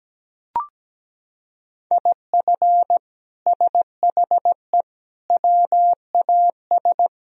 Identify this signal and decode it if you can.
Morse code, heard as a keyed tone, about 17 wpm, 700 Hz. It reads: IF SHE WAS
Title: Sentences from Top 1000 Words - Review Part 2 - 17wpm